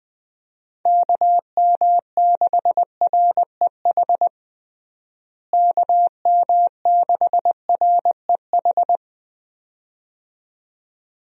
Morse code, heard as a keyed tone, about 20 wpm, 700 Hz. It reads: KM6REH KM6REH
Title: USA Call Signs - 6 Characters - Repeated 2x - 20wpm